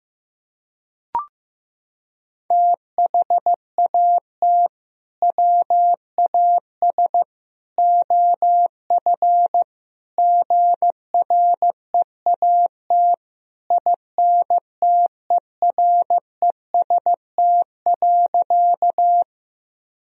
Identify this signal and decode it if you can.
Morse code, heard as a keyed tone, about 15 wpm, 700 Hz. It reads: THAT WAS OF GREAT INTEREST.
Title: Sentences from Top 1000 Words - Review Part 2 - 15wpm